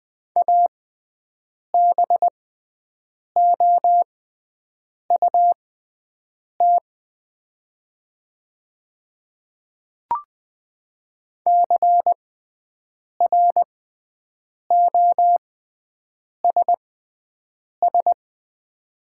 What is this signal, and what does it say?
Morse code, heard as a keyed tone, about 20 wpm, spaced as for 7 wpm, 700 Hz. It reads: ABOUT CROSS
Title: 6x Character Spacing - Top 300 Words - 20wpm